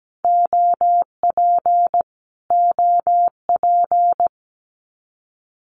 Morse code, heard as a keyed tone, about 17 wpm, 700 Hz. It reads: OP OP
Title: Abbreviations Repeated 3x - 17wpm